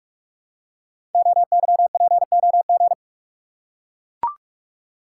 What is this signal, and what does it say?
Morse code, heard as a keyed tone, about 44 wpm, 700 Hz. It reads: OYPOG
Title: Code Groups - 5 Letters - 45wpm